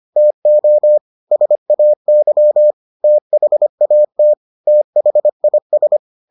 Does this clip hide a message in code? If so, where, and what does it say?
Yes: Morse code from 0.2 to 6.0 s, heard as a keyed tone, about 25 wpm, 600 Hz. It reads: TO SAY THAT THIS